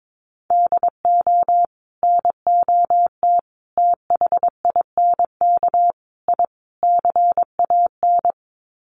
Morse code, heard as a keyed tone, about 22 wpm, 700 Hz. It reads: DO NOT THINK I CAN
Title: Sets of 5 Words Encore2 22wpm